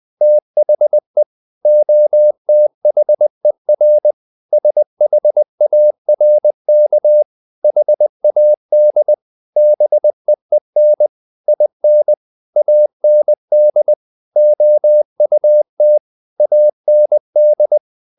Morse code, heard as a keyed tone, about 20 wpm, 600 Hz. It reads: THE OTHER SHARK HAD BEEN IN AND OUT AND
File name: the old man and the sea part 2 20wpm